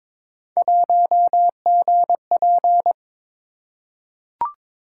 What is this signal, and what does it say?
Morse code, heard as a keyed tone, about 22 wpm, 700 Hz. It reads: 1GP
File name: practice / 22wpm / Three Letter-Number Combinations - 22wpm